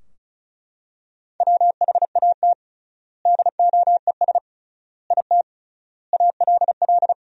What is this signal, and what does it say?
Morse code, heard as a keyed tone, about 36 wpm, 700 Hz. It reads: WHAT DOES IT ALL
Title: 3x Word-Spacing - Sentences from Top 200 Words - 35wpm